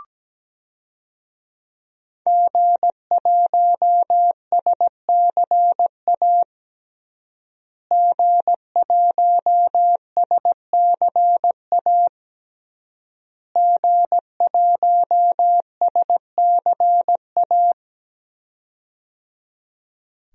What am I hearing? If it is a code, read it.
Morse code, heard as a keyed tone, about 17 wpm, 700 Hz. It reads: G1SCA G1SCA G1SCA
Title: DXCC-100 Call Signs - 5 Characters - Repeated 3x - 17wpm